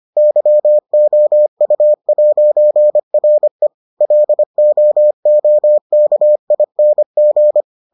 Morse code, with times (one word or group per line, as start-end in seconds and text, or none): 0.2-3.7 YOU'RE
4.0-7.6 LOOKING